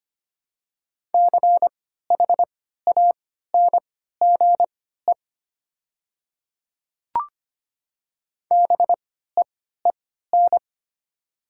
Morse code, heard as a keyed tone, about 25 wpm, spaced as for 14 wpm, 700 Hz. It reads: CHANGE BEEN
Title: 3x Character Spacing - Top 300 Words - 25wpm